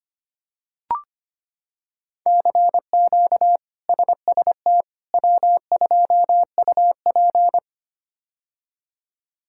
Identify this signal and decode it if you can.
Morse code, heard as a keyed tone, about 25 wpm, 700 Hz. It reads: CQ SST W2UP
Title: K1USN SST QSO Practice - 25wpm